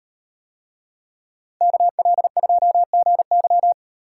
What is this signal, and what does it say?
Morse code, heard as a keyed tone, about 36 wpm, 700 Hz. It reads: KL2GY